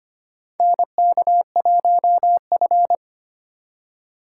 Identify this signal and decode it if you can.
Morse code, heard as a keyed tone, about 25 wpm, 700 Hz. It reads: NK1F
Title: USA Call Signs - 4 Characters - 3x Character Spacing - 25wpm